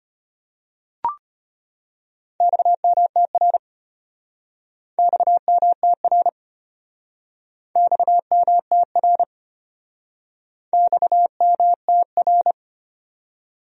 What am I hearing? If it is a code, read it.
Morse code, heard as a keyed tone, about 33 wpm, 700 Hz. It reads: XMTR XMTR XMTR XMTR